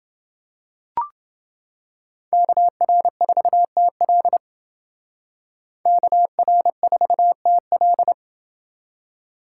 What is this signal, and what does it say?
Morse code, heard as a keyed tone, about 29 wpm, 700 Hz. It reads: KR4TL KR4TL